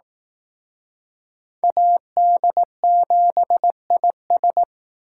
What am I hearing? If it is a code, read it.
Morse code, heard as a keyed tone, about 19 wpm, 700 Hz. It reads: AD7IS
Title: Speed Racing - USA 2x2 Call Signs 15wpm